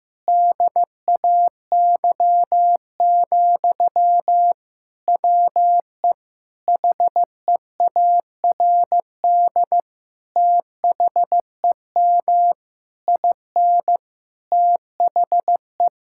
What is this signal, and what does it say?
Morse code, heard as a keyed tone, about 15 wpm, 700 Hz. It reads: DAY, WE HEARD THEM IN THE
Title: Sentences from Top 1000 Words - Review Part 5 - 15wpm